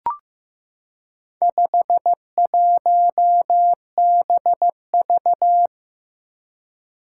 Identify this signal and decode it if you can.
Morse code, heard as a keyed tone, about 15 wpm, 700 Hz. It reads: 51BV
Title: Four Challenging Letter-Number Combinations 15wpm